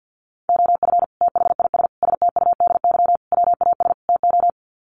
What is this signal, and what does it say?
Morse code, heard as a keyed tone, about 48 wpm, 700 Hz. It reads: OF THIS STUDY WAS TO
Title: Sets of 5 Words 50wpm